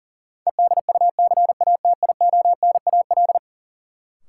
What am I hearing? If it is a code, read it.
Morse code, heard as a keyed tone, about 37 wpm, 700 Hz. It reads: EDUCATIONAL
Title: Sentences from Top 1200 Words - New Word Drill Alt - 40wpm